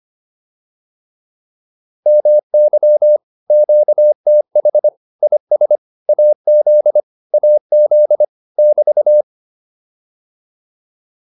Morse code, heard as a keyed tone, about 25 wpm, 600 Hz. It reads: MY QTH IS AZ AZ <BT>
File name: QSO209a 25